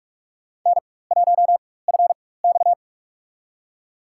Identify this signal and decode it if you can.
Morse code, heard as a keyed tone, about 46 wpm, spaced as for 21 wpm, 700 Hz. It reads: N1FX
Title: USA Call Signs - 4 Characters - 4x Character Spacing - 45wpm